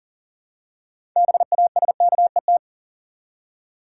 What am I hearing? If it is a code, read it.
Morse code, heard as a keyed tone, about 40 wpm, 700 Hz. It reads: BASKET
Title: Six Letter Words - 40wpm